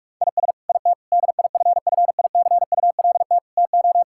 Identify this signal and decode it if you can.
Morse code, heard as a keyed tone, about 43 wpm, 700 Hz. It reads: IS IT DIFFICULT TO
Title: Sentences from Top 1000 Words - Review Part 5 - 45wpm